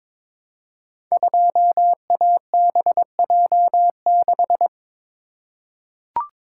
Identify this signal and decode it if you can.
Morse code, heard as a keyed tone, about 22 wpm, 700 Hz. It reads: 2ABJ6
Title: Code Groups - 5 Letter:Numbers - 22wpm